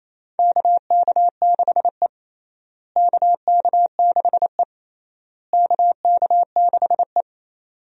Morse code, heard as a keyed tone, about 28 wpm, 700 Hz. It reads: KK6E KK6E KK6E
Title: USA Call Signs - 4 Characters - Repeated 3x - 28wpm